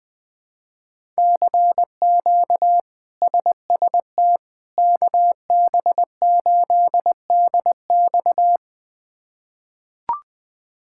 Morse code, heard as a keyed tone, about 20 wpm, 700 Hz. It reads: CQ SST KB8DX
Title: K1USN SST QSO Practice - 20wpm